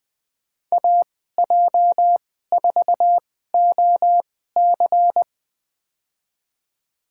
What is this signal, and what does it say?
Morse code, heard as a keyed tone, about 20 wpm, spaced as for 15 wpm, 700 Hz. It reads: AJ4OC
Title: USA Call Signs - 5 Characters - 2x Character Spacing - 20wpm